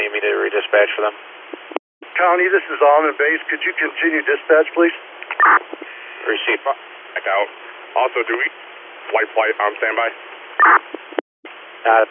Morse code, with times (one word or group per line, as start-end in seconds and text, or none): none